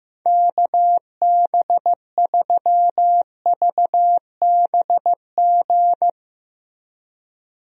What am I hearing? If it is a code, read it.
Morse code, heard as a keyed tone, about 15 wpm, 700 Hz. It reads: KB3VBG
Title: USA Call Signs - 6 Characters - Repeated 3x - 15wpm